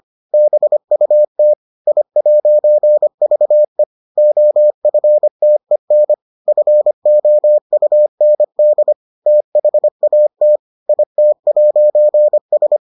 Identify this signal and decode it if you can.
Morse code, heard as a keyed tone, about 25 wpm, 600 Hz. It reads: BUT I'VE OFTEN FOUND THAT IT'S